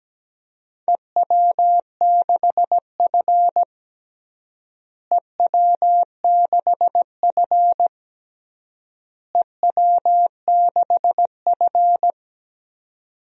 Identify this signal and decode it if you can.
Morse code, heard as a keyed tone, about 17 wpm, 700 Hz. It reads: EW6F EW6F EW6F
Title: DXCC-100 Call Signs - 4 Characters - Repeated 3x - 17wpm